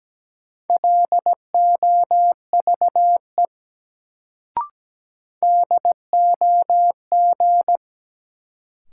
Morse code, heard as a keyed tone, about 17 wpm, 700 Hz. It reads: LOVE DOG
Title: Sentences from Top 400 Words - New Word Drill - 17wpm